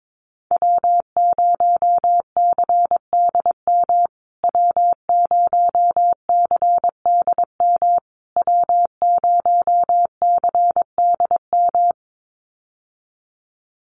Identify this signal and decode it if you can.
Morse code, heard as a keyed tone, about 22 wpm, 700 Hz. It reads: W0CDM W0CDM W0CDM
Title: USA Call Signs Repeated 3x 22wpm